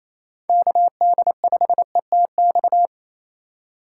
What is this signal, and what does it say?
Morse code, heard as a keyed tone, about 28 wpm, 690 Hz. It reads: KD5ETX
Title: USA Call Signs - 6 Characters - 4x Character Spacing - 28wpm